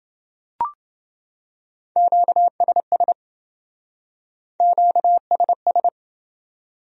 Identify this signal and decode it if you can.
Morse code, heard as a keyed tone, about 28 wpm, 700 Hz. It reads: QSS QSS